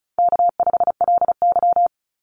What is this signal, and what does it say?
Morse code, heard as a keyed tone, about 36 wpm, 700 Hz. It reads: K5LY
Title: Call Signs - 35wpm